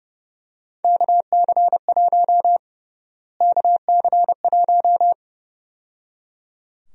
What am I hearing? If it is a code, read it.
Morse code, heard as a keyed tone, about 30 wpm, 700 Hz. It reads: KC1 KC1
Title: USA Call Sign Prefixes - Repeated 2x - 30wpm